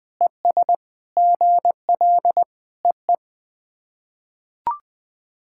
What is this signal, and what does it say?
Morse code, heard as a keyed tone, about 21 wpm, 700 Hz. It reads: ES GL EE
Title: SOTA QSO Practice 20wpm